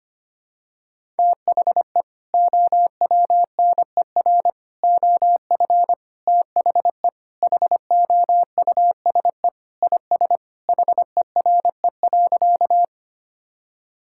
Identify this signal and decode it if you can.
Morse code, heard as a keyed tone, about 25 wpm, 700 Hz. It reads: THE OWNER OF THE HOUSE IS HERE.
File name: practice / 25wpm / Sentences with 3-Word Phrases - 25wpm